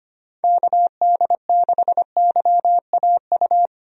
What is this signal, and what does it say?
Morse code, heard as a keyed tone, about 25 wpm, 700 Hz. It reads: KD6YAU
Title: USA Call Signs - 6 Characters - 25wpm